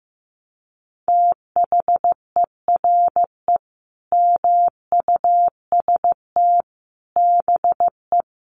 Morse code, heard as a keyed tone, about 14 wpm, 700 Hz. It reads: THERE MUST BE
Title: Sentences from Top 700 Words 15wpm